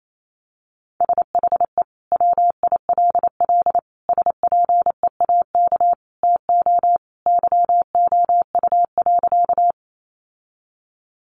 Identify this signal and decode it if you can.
Morse code, heard as a keyed tone, about 28 wpm, 700 Hz. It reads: SHE WILL SPEAK TO YOU.
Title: Sentences from Top 1100 Words 28wpm